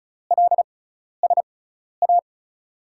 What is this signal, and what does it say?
Morse code, heard as a keyed tone, about 36 wpm, spaced as for 12 wpm, 700 Hz. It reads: LSA
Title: USA Call Signs - 5 Characters - 6x Character Spacing - 35wpm